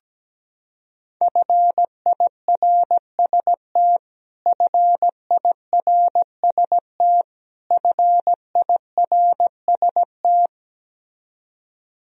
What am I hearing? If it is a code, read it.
Morse code, heard as a keyed tone, about 17 wpm, 700 Hz. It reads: FIRST FIRST FIRST
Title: Top 100 Words Repeated 17wpm